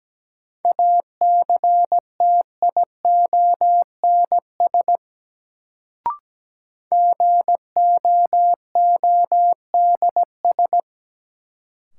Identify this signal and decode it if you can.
Morse code, heard as a keyed tone, about 17 wpm, 700 Hz. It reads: ACTIONS GOODS
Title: Sentences from Top 1000 Words - Review Part 6 - New Word Drill - 17wpm